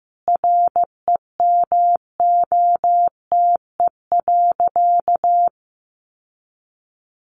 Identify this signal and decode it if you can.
Morse code, heard as a keyed tone, about 15 wpm, 700 Hz. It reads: REMOTE.
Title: Sentences from Top 1300 Words 15wpm